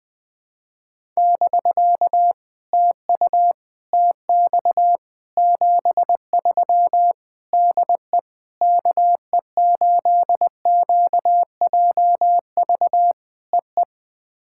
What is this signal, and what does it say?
Morse code, heard as a keyed tone, about 20 wpm, 690 Hz. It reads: <BK> TU TX 73 DE KE8QJV EE
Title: POTA QSO Practice - 20wpm